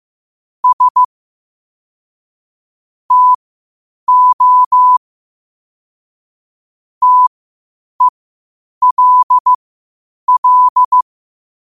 Morse code, heard as a keyed tone, about 15 wpm, spaced as for 8 wpm, 1000 Hz. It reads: S TO TELL
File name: qotdfiles / 8wpm